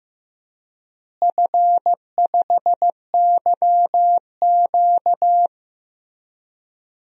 Morse code, heard as a keyed tone, about 15 wpm, 700 Hz. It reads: F5YQ
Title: DXCC-100 Call Signs - 4 Characters - Repeated 3x - 15wpm